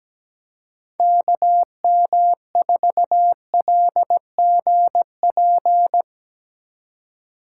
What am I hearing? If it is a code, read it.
Morse code, heard as a keyed tone, about 17 wpm, 700 Hz. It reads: KM4LGP
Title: USA Call Signs - 6 Characters - 3x Character Spacing - 17wpm